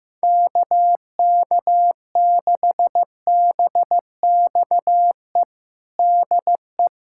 Morse code, heard as a keyed tone, about 15 wpm, 700 Hz. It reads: KK6BXE DE
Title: USA Callsign DE Callsign - 6 Characters - 15wpm